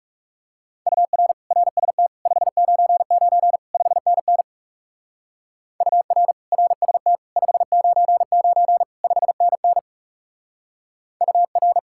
Morse code, heard as a keyed tone, about 36 wpm, 700 Hz. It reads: UR RST 599 5NN UR RST 599 5NN UR
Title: Speed-Racing Common QSO Phrases - 30wpm